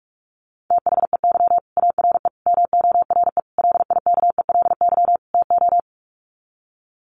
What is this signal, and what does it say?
Morse code, heard as a keyed tone, about 44 wpm, 700 Hz. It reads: THEY ARE MORE LIKELY TO
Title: Sets of 5 Words Encore2 45wpm